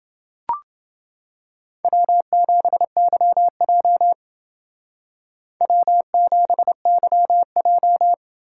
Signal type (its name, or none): Morse code